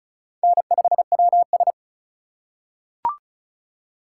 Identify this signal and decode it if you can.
Morse code, heard as a keyed tone, about 36 wpm, 700 Hz. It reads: N5WS
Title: USA Call Signs - 4 Characters - 3x Character Spacing - 35wpm